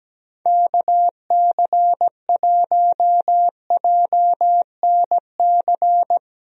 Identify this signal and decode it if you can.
Morse code, heard as a keyed tone, about 17 wpm, 700 Hz. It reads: KC1JNC
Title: USA Call Signs - 6 Characters - Repeated 2x - 17wpm